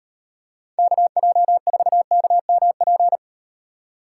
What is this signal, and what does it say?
Morse code, heard as a keyed tone, about 38 wpm, 700 Hz. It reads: KJ4KMP